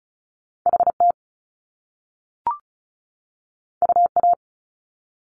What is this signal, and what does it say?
Morse code, heard as a keyed tone, about 36 wpm, 700 Hz. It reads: HT UA
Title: Top Two Letter Combinations in Words 35wpm